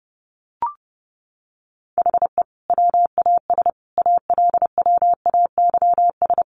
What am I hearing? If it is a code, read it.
Morse code, heard as a keyed tone, about 30 wpm, 700 Hz. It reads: HE WAS ALWAYS
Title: Sentences from Top 1300 Words 30wpm